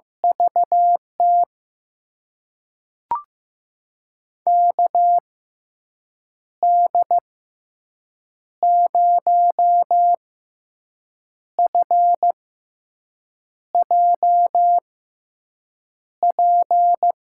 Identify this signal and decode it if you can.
Morse code, heard as a keyed tone, about 15 wpm, spaced as for 6 wpm, 700 Hz. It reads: VT KD0FJP
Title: USA Call Signs - 6 Characters - 6x Character Spacing - 15wpm